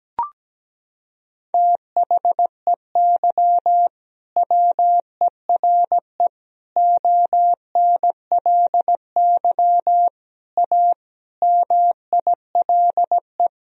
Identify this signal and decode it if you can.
Morse code, heard as a keyed tone, about 17 wpm, 700 Hz. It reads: THEY WERE ONLY A MILE
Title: Sentences from Top 1000 Words - Review Part 1 - 17wpm